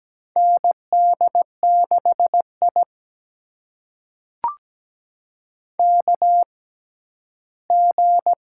Morse code, heard as a keyed tone, about 17 wpm, spaced as for 11 wpm, 700 Hz. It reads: ND6I KG
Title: USA Call Signs - 4 Characters - 6x Character Spacing - 17wpm